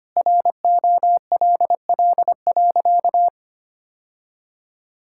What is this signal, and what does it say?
Morse code, heard as a keyed tone, about 25 wpm, 700 Hz. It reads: ROLL.
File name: practice / 25wpm / All-Age Appropriate Jokes - 25wpm